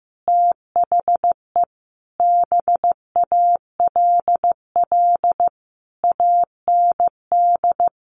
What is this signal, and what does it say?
Morse code, heard as a keyed tone, about 15 wpm, 700 Hz. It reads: THE BALL AND
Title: Sentences from Top 900 Words 15wpm